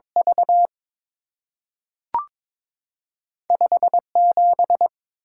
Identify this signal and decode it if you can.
Morse code, heard as a keyed tone, about 22 wpm, 700 Hz. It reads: V 5 7
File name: practice / 22wpm / Two Letter-Number Combinations - 22wpm